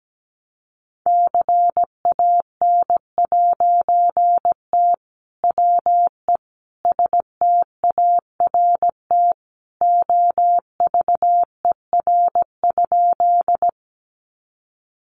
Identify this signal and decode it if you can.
Morse code, heard as a keyed tone, about 17 wpm, 700 Hz. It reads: CAN'T WE START OVER?
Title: A Sound Of Thunder 17wpm